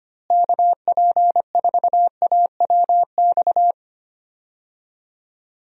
Morse code, heard as a keyed tone, about 25 wpm, 700 Hz. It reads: KP4AWX
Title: DXCC-100 Call Signs - 6 Characters - Repeated 3x - 25wpm